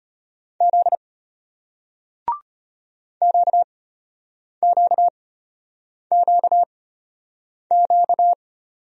Morse code, heard as a keyed tone, about 34 wpm, spaced as for 8 wpm, 700 Hz. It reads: Z QQQQ